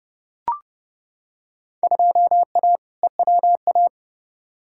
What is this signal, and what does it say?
Morse code, heard as a keyed tone, about 30 wpm, 700 Hz. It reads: 2A EWA
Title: Field Day Exchanges - 30wpm